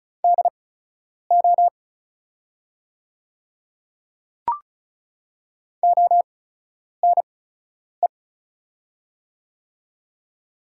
Morse code, heard as a keyed tone, about 33 wpm, spaced as for 9 wpm, 690 Hz. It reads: DO ONE
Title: 8x Character Spacing - Top 300 Words - 35wpm